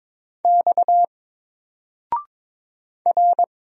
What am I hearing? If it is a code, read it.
Morse code, heard as a keyed tone, about 22 wpm, 700 Hz. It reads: X R
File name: Single-Letter-Number-22wpm